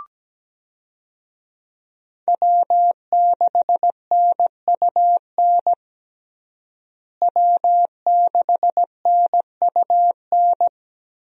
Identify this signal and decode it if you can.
Morse code, heard as a keyed tone, about 17 wpm, 700 Hz. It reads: W6NUN W6NUN
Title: USA Call Signs - 5 Characters - Repeated 2x - 17wpm